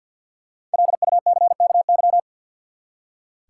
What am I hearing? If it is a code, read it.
Morse code, heard as a keyed tone, about 46 wpm, 700 Hz. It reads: LUCKY
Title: Five Letter Words - 50wpm